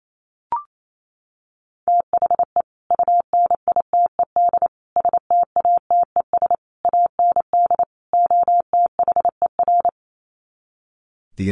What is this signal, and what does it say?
Morse code, heard as a keyed tone, about 28 wpm, 700 Hz. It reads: THE UNITED STATES AND OTHER
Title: Sets of 5 Words Encore 28wpm